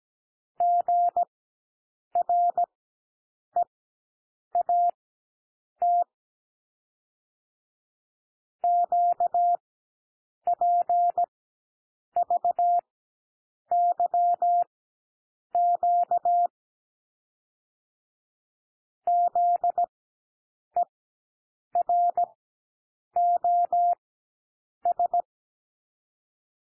Morse code, heard as a keyed tone, about 17 wpm, spaced as for 7 wpm, 700 Hz. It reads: GREAT QPVYQ ZEROS
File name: SOUND LETTERS 17 wpm